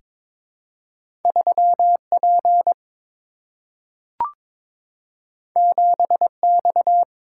Morse code, heard as a keyed tone, about 22 wpm, 700 Hz. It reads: 3 P 7 X
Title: Single-Letter Single-Number 22wpm